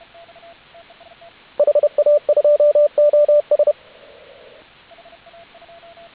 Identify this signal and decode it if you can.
Morse code, heard as a keyed tone, about 32 wpm, 590 Hz, with radio noise. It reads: HA2OS